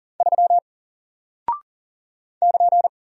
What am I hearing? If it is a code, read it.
Morse code, heard as a keyed tone, about 40 wpm, 700 Hz. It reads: 3 <KN>
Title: Single Character (Letter:Number:Punctuation:Prosign) Alt 40wpm